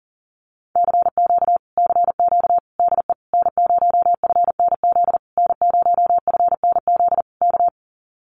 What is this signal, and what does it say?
Morse code, heard as a keyed tone, about 39 wpm, 700 Hz. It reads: CQ CQ DE N0FNZ N0FNZ K
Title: Common QSO Phrases Encore 40wpm